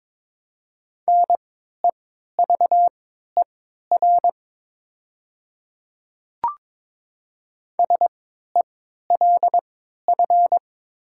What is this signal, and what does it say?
Morse code, heard as a keyed tone, about 22 wpm, spaced as for 12 wpm, 700 Hz. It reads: NEVER SELF
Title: 3x Character Spacing - Top 500 Words - 22wpm